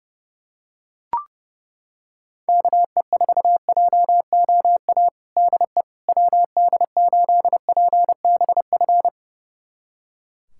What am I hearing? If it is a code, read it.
Morse code, heard as a keyed tone, about 30 wpm, 700 Hz. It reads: KE4JOA DE WD8PBF
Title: USA Callsign DE Callsign - 6 Characters - 30wpm